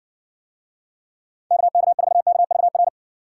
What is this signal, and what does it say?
Morse code, heard as a keyed tone, about 60 wpm, 700 Hz. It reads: KK4KVR